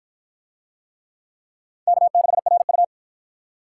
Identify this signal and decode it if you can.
Morse code, heard as a keyed tone, about 52 wpm, 700 Hz. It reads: K6RU